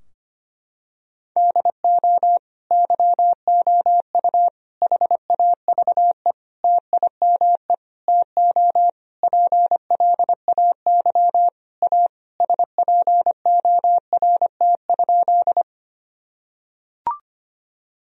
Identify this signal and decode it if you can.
Morse code, heard as a keyed tone, about 25 wpm, 700 Hz. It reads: DO YOU HAVE TIME TO PLAY A SPORT?
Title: Sentences from Top 1000 Words - Review Part 4 - 25wpm